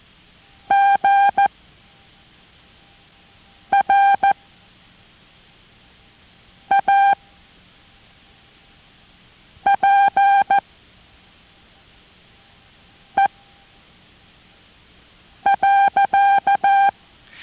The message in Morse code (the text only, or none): GRAPE.